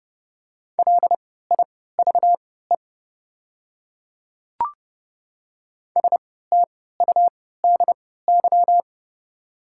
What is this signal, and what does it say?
Morse code, heard as a keyed tone, about 30 wpm, spaced as for 17 wpm, 700 Hz. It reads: LIVE STUDY
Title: 3x Character Spacing - Top 500 Words - 30wpm